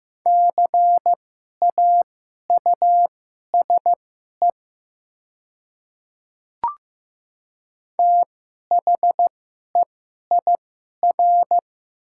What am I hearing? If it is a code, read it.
Morse code, heard as a keyed tone, about 15 wpm, spaced as for 11 wpm, 700 Hz. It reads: CAUSE THEIR
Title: 2x Character Spacing - Top 300 Words - 15wpm